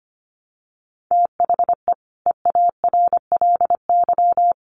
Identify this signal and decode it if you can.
Morse code, heard as a keyed tone, about 25 wpm, 700 Hz. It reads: THE EARLY
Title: Sets of 5 Words Encore 25wpm